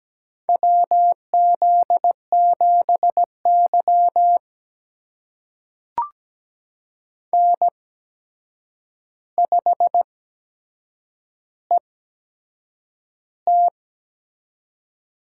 Morse code, heard as a keyed tone, about 17 wpm, spaced as for 7 wpm, 700 Hz. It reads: WZ7Y N5ET
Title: USA Call Signs - 4 Characters - 8x Character Spacing - 17wpm